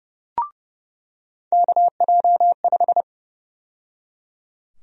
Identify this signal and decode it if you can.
Morse code, heard as a keyed tone, about 29 wpm, 700 Hz. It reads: KJ5
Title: USA Call Sign Prefixes - 30wpm